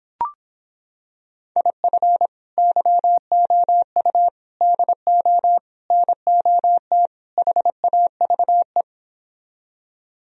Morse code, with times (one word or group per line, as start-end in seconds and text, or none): 1.6-2.3 IF
2.6-4.3 YOU
4.6-5.6 DO
5.9-7.1 NOT
7.4-8.8 HAVE